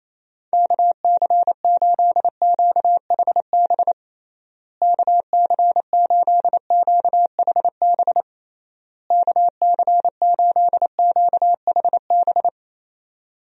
Morse code, heard as a keyed tone, about 27 wpm, 700 Hz. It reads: KC8QHB KC8QHB KC8QHB
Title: USA Call Signs - 6 Characters - Repeated 3x - 28wpm